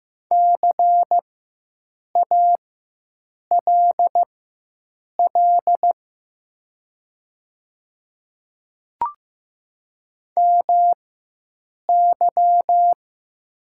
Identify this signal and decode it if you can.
Morse code, heard as a keyed tone, about 15 wpm, spaced as for 7 wpm, 700 Hz. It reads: CALL MY
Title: 4x Character Spacing - Top 100 Words - 15wpm